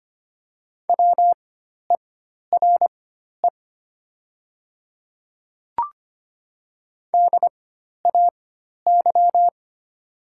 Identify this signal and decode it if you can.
Morse code, heard as a keyed tone, about 25 wpm, spaced as for 12 wpm, 700 Hz. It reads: WERE DAY
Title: 4x Character Spacing - Top 100 Words - 25wpm